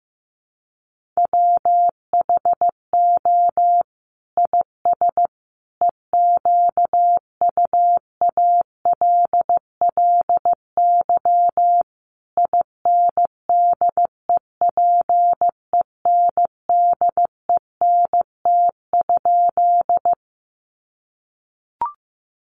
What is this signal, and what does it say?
Morse code, heard as a keyed tone, about 15 wpm, 700 Hz. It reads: WHO IS EQUALLY INDEPENDENT?
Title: Sentences from Top 1400 Words 15wpm